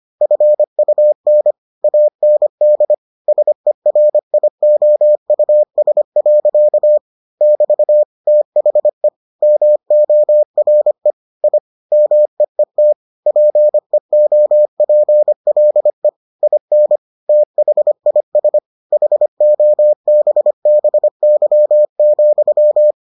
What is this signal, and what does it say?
Morse code, heard as a keyed tone, about 25 wpm, 600 Hz. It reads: FUN AND SERIOUS. <BT> THE MORE I MEET PEOPLE IN THIS HOBBY,